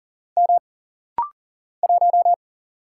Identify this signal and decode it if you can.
Morse code, heard as a keyed tone, about 40 wpm, 700 Hz. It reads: M 1
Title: Single Letter-Number - Rapid-Fire 40wpm